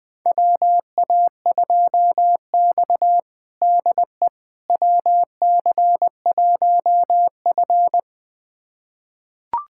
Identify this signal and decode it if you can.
Morse code, heard as a keyed tone, about 20 wpm, 700 Hz. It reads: WA2X DE WC1F
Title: USA Callsign DE Callsign - 4 Characters - 20wpm